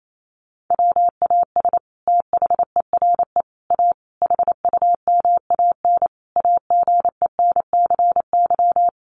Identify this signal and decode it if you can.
Morse code, heard as a keyed tone, about 28 wpm, 700 Hz. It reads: WAS THERE A HUMAN AGENCY
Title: Sentences from Top 1300 Words 28wpm